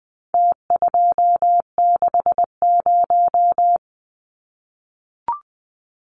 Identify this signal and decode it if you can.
Morse code, heard as a keyed tone, about 20 wpm, 700 Hz. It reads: T260
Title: Numbers - Contest Style - 4-Digit 20wpm